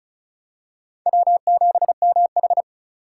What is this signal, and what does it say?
Morse code, heard as a keyed tone, about 34 wpm, 700 Hz. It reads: W7MH
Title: USA Call Signs - 4 Characters - 2x Character Spacing - 35wpm